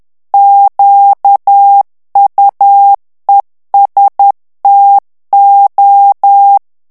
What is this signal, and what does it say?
Morse code, heard as a keyed tone, about 10 wpm, 800 Hz. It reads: QUESTO
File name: morse mp3 / ITALIEN texte clair 1 vitesse 10 mots minute